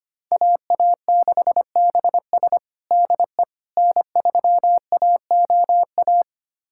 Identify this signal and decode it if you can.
Morse code, heard as a keyed tone, about 25 wpm, 700 Hz. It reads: AA6BS DE N3AOA
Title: USA Callsign DE Callsign - 5 Characters - 25wpm